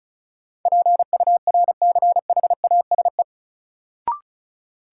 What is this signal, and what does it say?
Morse code, heard as a keyed tone, about 34 wpm, 700 Hz. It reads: PURCHASE
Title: Sentences from Top 800 Words - New Word Drill - 35wpm